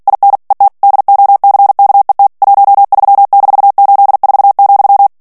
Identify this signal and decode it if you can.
Morse code, heard as a keyed tone, about 48 wpm, 800 Hz. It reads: IN ET DOCKET 13-84,
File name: ANGLAIS texte clair 3 vitesse 45 mots minute